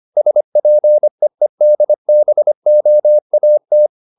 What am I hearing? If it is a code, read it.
Morse code, heard as a keyed tone, about 25 wpm, 600 Hz. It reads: SPEEDBOAT